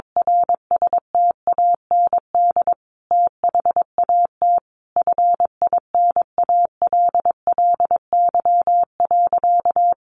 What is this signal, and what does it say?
Morse code, heard as a keyed tone, about 22 wpm, 700 Hz. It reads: RSTAND THAT FINALLY.
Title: Sentences from Top 1000 Words 22wpm